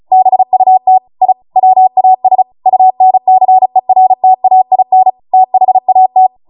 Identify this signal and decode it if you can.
Morse code, heard as a keyed tone, about 36 wpm, 750 Hz. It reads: BUT I WAS UNCERTAIN THAT